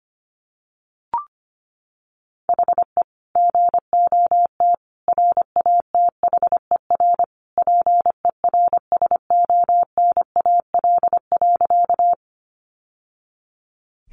Morse code, heard as a keyed tone, about 25 wpm, 700 Hz. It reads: HE GOT RATHER PERSONAL.
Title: Sentences from Top 700 Words 25wpm